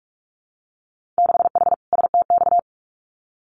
Morse code, heard as a keyed tone, about 46 wpm, 700 Hz. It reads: 6H STX
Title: Winter Field Day Exchanges Encore 45wpm